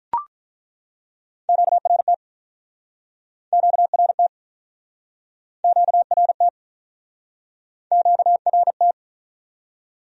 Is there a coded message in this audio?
Yes: Morse code from 1.5 to 8.9 s, heard as a keyed tone, about 48 wpm, 700 Hz. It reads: QRT QRT QRT QRT